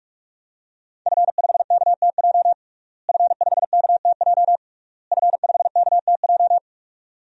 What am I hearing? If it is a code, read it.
Morse code, heard as a keyed tone, about 41 wpm, 700 Hz. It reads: F5KTJ F5KTJ F5KTJ
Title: DXCC-100 Call Signs - 5 Characters - Repeated 3x - 45wpm